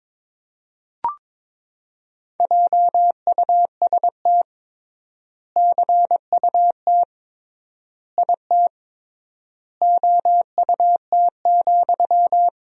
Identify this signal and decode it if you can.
Morse code, heard as a keyed tone, about 22 wpm, 700 Hz. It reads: JUST CUT IT OUT,
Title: 3x Word-Spacing - Sentences from Top 300 Words - 22wpm